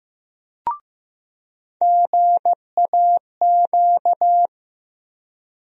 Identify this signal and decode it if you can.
Morse code, heard as a keyed tone, about 15 wpm, 700 Hz. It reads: GAQ
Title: USA Call Sign Postfixes - 3 Letters 15wpm